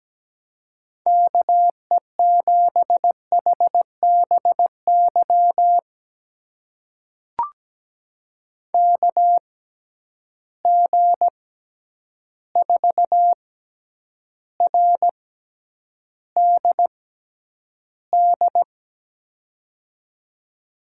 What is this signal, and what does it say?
Morse code, heard as a keyed tone, about 17 wpm, spaced as for 9 wpm, 700 Hz. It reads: KE7HBY KG4RDD
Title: USA Call Signs - 6 Characters - 6x Character Spacing - 17wpm